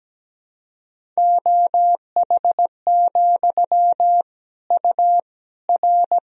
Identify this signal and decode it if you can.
Morse code, heard as a keyed tone, about 17 wpm, 700 Hz. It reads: OH, U R
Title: Sentences from Top 500 Words and Abbreviations 17wpm